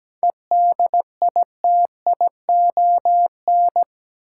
Morse code, heard as a keyed tone, about 17 wpm, 700 Hz. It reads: EDITION
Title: Sets of 5 Words - New Word Drill - 17wpm